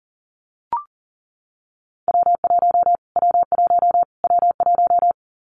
Morse code, heard as a keyed tone, about 40 wpm, 700 Hz. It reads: W1 W1 W1
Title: Callsign Prefixes for Top 25 DXCC - Repeated 3x - 40wpm